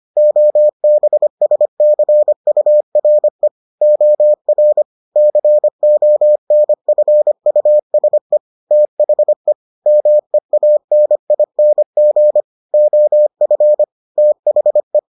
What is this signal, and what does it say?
Morse code, heard as a keyed tone, about 25 wpm, 600 Hz. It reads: OBSCURE OR CONFUSE THE MEANING OF THE